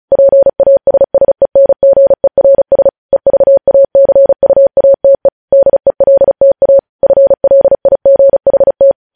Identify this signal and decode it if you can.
Morse code, heard as a keyed tone, about 36 wpm, 550 Hz. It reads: PASSENGERS EVACUATE DELTA FLIGHT
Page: news headlines in morse code at 35 wpm